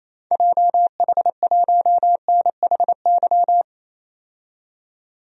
Morse code, heard as a keyed tone, about 28 wpm, 700 Hz. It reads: JH1NHY
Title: DXCC-100 Call Signs - 6 Characters - 28wpm